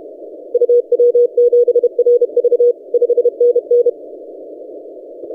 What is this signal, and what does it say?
Morse code, heard as a keyed tone, about 32 wpm, 480 Hz. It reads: UW7RV 5NN
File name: posts / ri0x-20cw- -0519z